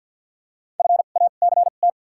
Morse code, heard as a keyed tone, about 53 wpm, 700 Hz. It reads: F A C T
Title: 2x Character Spacing - Top 500 Words - 50wpm